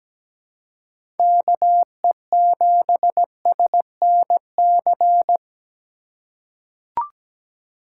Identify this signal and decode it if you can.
Morse code, heard as a keyed tone, about 17 wpm, 700 Hz. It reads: KE7SNC
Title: USA Call Signs - 4x Character Spacing - 17wpm